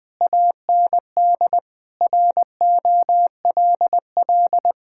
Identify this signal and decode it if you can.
Morse code, heard as a keyed tone, about 21 wpm, 700 Hz. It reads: AND ROLL
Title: Sentences with 6-Word Phrases - Spoken Beginning - 20wpm